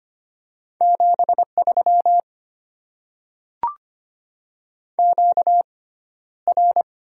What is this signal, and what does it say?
Morse code, heard as a keyed tone, about 25 wpm, spaced as for 13 wpm, 700 Hz. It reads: 73 QR
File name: Top 20 QSO Elements - 6x Character Spacing - 25wpm